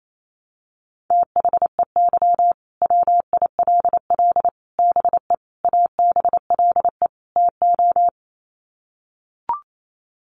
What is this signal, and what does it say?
Morse code, heard as a keyed tone, about 28 wpm, 700 Hz. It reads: THEY WILL BE ABLE TO
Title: Sets of 5 Words Encore 28wpm